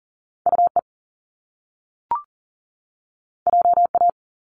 Morse code, heard as a keyed tone, about 41 wpm, 700 Hz. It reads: UE JA
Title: Top Two Letter Combinations in Words 40wpm